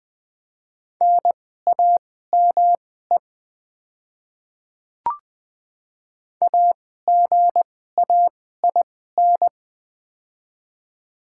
Morse code, heard as a keyed tone, about 20 wpm, spaced as for 15 wpm, 700 Hz. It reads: NAME AGAIN
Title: 2x Character Spacing - Top 200 Words - 20wpm